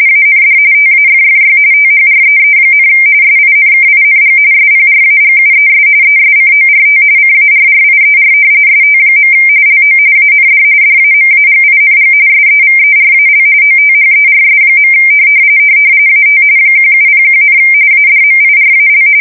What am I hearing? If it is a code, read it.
RTTY, about 45.45 baud. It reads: AND TAFFY THREE, SINKING SPRAGUES ESCORT CARRIER ST. LO AND DAMAGING SEVERAL OTHERS. AT 1701, ADMIRAL
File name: fsk170 / battleshipnj